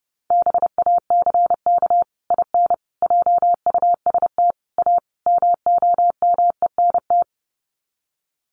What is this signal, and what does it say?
Morse code, heard as a keyed tone, about 30 wpm, 700 Hz. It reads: BACK IN JUST A MOMENT
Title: Sets of 5 Words Encore 30wpm